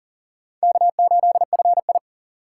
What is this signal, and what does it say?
Morse code, heard as a keyed tone, about 53 wpm, spaced as for 47 wpm, 700 Hz. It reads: K8FI
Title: Speed Racing - USA 1x2 Call Signs 40wpm